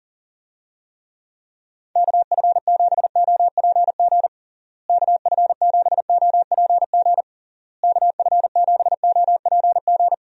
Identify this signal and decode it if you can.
Morse code, heard as a keyed tone, about 40 wpm, 700 Hz. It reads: KF7OPG KF7OPG KF7OPG
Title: USA Call Signs - 6 Characters - Repeated 3x - 40wpm